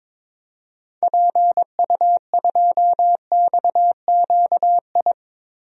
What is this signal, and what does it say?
Morse code, heard as a keyed tone, about 21 wpm, 700 Hz. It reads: PU2XQI
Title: DXCC-100 Call Signs - 6 Characters - 22wpm